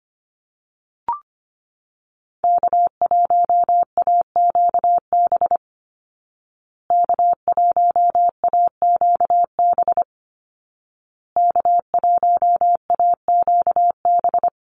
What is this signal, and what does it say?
Morse code, heard as a keyed tone, about 25 wpm, 700 Hz. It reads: K1AQB K1AQB K1AQB
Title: Call Signs Repeated 2x - 25wpm